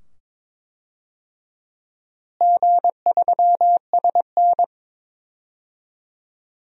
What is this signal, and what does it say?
Morse code, heard as a keyed tone, about 21 wpm, 700 Hz. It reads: G3SN
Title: DXCC-100 Call Signs - 4 Characters - Repeated 2x - 22wpm